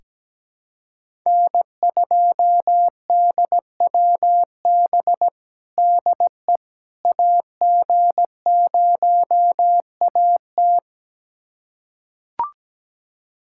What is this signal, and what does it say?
Morse code, heard as a keyed tone, about 17 wpm, 700 Hz. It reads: N2DWB DE AG0AT
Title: USA Callsign DE Callsign - 5 Characters - 17wpm